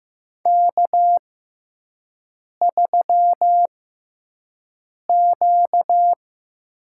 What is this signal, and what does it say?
Morse code, heard as a keyed tone, about 15 wpm, spaced as for 5 wpm, 700 Hz. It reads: K3Q
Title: USA Call Signs - 6x Character Spacing - 15wpm